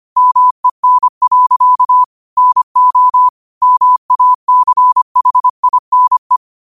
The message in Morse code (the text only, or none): MEN. NO MACHINE